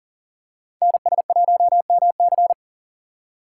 Morse code, heard as a keyed tone, about 42 wpm, 700 Hz. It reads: NS1MC
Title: USA Call Signs - 5 Characters - 3x Character Spacing - 40wpm